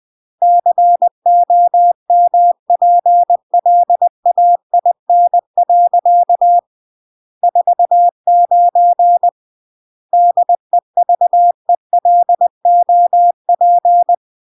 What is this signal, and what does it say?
Morse code, heard as a keyed tone, about 20 wpm, 700 Hz. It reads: COMPLAIN. 49 DEVELOP